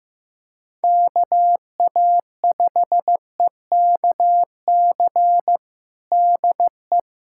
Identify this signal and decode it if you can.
Morse code, heard as a keyed tone, about 15 wpm, 700 Hz. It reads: KA5EKC DE
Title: USA Callsign DE Callsign - 6 Characters - 15wpm